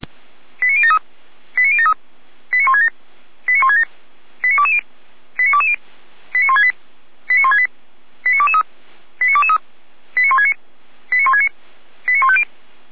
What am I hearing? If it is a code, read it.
Five-tone selective call (ZVEI), heard as quick runs of five tones, tones of about 70 ms. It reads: 89073 89073 89167 89167 89220 89220 89168 89168 89223 89223 89169 89169 89150